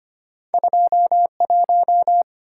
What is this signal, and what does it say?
Morse code, heard as a keyed tone, about 25 wpm, 700 Hz. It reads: 21
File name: Two Letter-Number Combinations - 25wpm